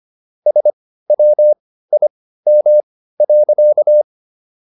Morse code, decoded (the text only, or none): SWIM.